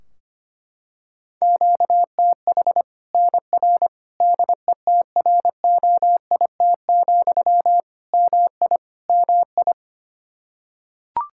Morse code, heard as a keyed tone, about 25 wpm, 700 Hz. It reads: QTH NR DETROIT, MI MI
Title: Practice QSO - Exchange 1 - Beginning - 25wpm